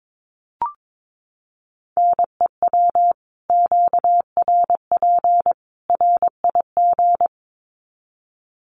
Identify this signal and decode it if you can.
Morse code, heard as a keyed tone, about 22 wpm, 700 Hz. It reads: NEW QRP RIG
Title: Common QSO Phrases 22wpm